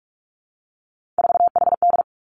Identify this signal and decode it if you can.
Morse code, heard as a keyed tone, about 46 wpm, 700 Hz. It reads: 4HD
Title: Three Challenging Letter-Number Combinations 45wpm